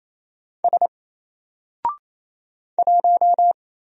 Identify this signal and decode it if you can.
Morse code, heard as a keyed tone, about 27 wpm, 700 Hz. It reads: S 1
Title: Single Character (Letter:Number:Punctuation:Prosign) Alt 28wpm